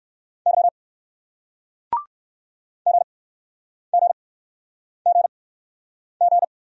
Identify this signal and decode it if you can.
Morse code, heard as a keyed tone, about 60 wpm, spaced as for 9 wpm, 700 Hz. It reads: Y GGGG